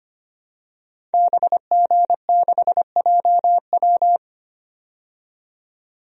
Morse code, heard as a keyed tone, about 25 wpm, 700 Hz. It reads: BG6JW
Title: DXCC-100 Call Signs - 5 Characters - 25wpm